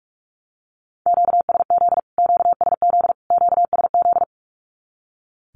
Morse code, heard as a keyed tone, about 44 wpm, 700 Hz. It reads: QSZ QSZ QSZ
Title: Q-Codes Repeated 3x 45wpm